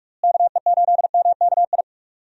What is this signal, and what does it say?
Morse code, heard as a keyed tone, about 46 wpm, 700 Hz. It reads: KE8MKI
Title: USA Call Signs - 6 Characters - Repeated 3x - 45wpm